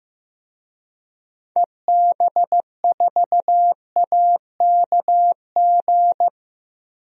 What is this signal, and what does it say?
Morse code, heard as a keyed tone, about 15 wpm, 700 Hz. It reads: EB4AKG
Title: DXCC-100 Call Signs - 6 Characters - 15wpm